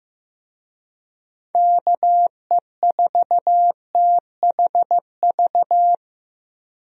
Morse code, heard as a keyed tone, about 15 wpm, 700 Hz. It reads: KE4THV
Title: USA Call Signs - 6 Characters - 15wpm